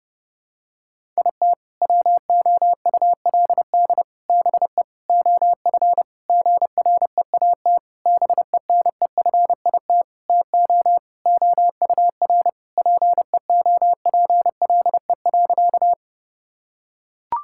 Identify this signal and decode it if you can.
Morse code, heard as a keyed tone, about 30 wpm, 700 Hz. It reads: IT WOULD BE OF GREAT BENEFIT TO OUR PEOPLE.
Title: Sentences from Top 1000 Words - Review Part 6 - 30wpm